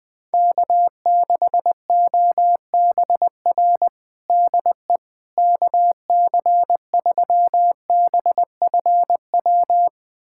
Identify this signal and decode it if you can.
Morse code, heard as a keyed tone, about 20 wpm, 700 Hz. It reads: K6OBR DE KC3BFW
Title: USA Callsign DE Callsign 20wpm